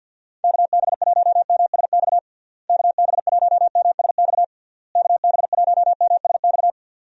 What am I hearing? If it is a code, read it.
Morse code, heard as a keyed tone, about 51 wpm, 700 Hz. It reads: KB1MSX KB1MSX KB1MSX
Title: USA Call Signs - 6 Characters - Repeated 3x - 50wpm